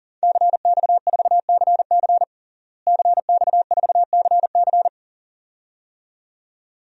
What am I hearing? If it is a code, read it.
Morse code, heard as a keyed tone, about 40 wpm, 700 Hz. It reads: CX4CC CX4CC
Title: DXCC-100 Call Signs - 5 Characters - Repeated 2x - 40wpm